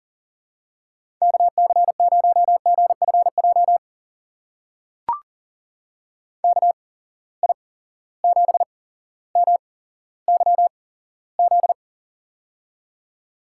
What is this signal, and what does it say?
Morse code, heard as a keyed tone, about 40 wpm, spaced as for 17 wpm, 700 Hz. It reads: KC0GFJ KI7MYZ
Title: USA Call Signs - 6 Characters - 8x Character Spacing - 40wpm